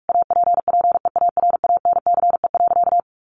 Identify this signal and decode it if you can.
Morse code, heard as a keyed tone, about 44 wpm, 700 Hz. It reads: APPEARANCE.
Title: Sentences from Top 1100 Words 45wpm